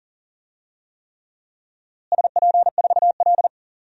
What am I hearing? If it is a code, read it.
Morse code, heard as a keyed tone, about 38 wpm, 700 Hz. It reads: SP4L
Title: DXCC-100 Call Signs - 4 Characters - Repeated 3x - 40wpm